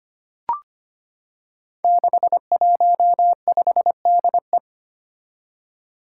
Morse code, heard as a keyed tone, about 25 wpm, 700 Hz. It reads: 615DE
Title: Five Challenging Letter-Number Combinations 25wpm